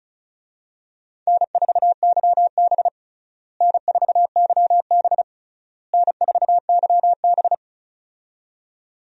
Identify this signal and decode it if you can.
Morse code, heard as a keyed tone, about 36 wpm, 700 Hz. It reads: N4YB N4YB N4YB
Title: USA Call Signs - 4 Characters - Repeated 3x - 35wpm